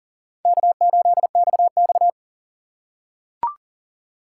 Morse code, heard as a keyed tone, about 40 wpm, 700 Hz. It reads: K8XX
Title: USA Call Signs - 3x Character Spacing - 40wpm